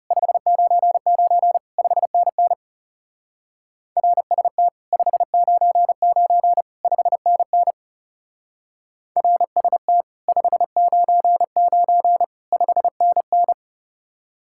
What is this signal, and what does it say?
Morse code, heard as a keyed tone, about 39 wpm, 700 Hz. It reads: 599 5NN RST 599 5NN RST 599 5NN